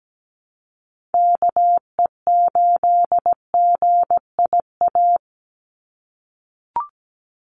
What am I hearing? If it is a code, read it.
Morse code, heard as a keyed tone, about 17 wpm, 700 Hz. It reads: KE8GIA
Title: USA Call Signs 17wpm